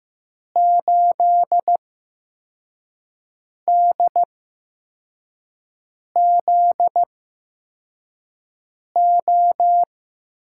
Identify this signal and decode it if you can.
Morse code, heard as a keyed tone, about 15 wpm, spaced as for 4 wpm, 700 Hz. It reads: 8DZO
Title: USA Call Signs - 6 Characters - 8x Character Spacing - 15wpm